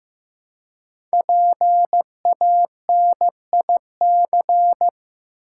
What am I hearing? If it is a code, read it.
Morse code, heard as a keyed tone, about 15 wpm, 700 Hz. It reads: PANIC
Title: Five Letter Words - 15wpm